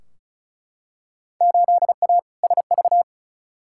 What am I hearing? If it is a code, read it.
Morse code, heard as a keyed tone, about 36 wpm, 700 Hz. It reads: 8A SV
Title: Field Day Exchanges Encore - 35wpm